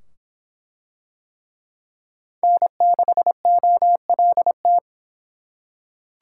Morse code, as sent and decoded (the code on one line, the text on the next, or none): -. -.... --- .-.. -
N6OLT